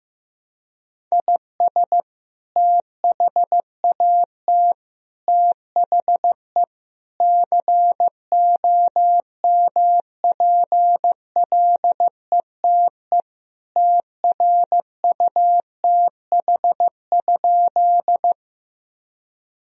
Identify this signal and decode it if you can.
Morse code, heard as a keyed tone, about 15 wpm, 700 Hz. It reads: IS THAT THE COMPLETE TRUTH?
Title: Sentences from Top 1000 Words - Review Part 2 - 15wpm